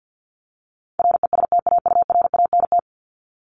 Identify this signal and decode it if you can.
Morse code, heard as a keyed tone, about 50 wpm, 700 Hz. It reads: RESTAURANT
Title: Top 1751-2000 Words 50wpm